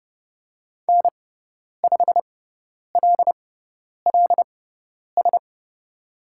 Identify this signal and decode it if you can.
Morse code, heard as a keyed tone, about 29 wpm, spaced as for 10 wpm, 700 Hz. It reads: N5LLS
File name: Lesson 16g - Letter c - Call Signs 30wpm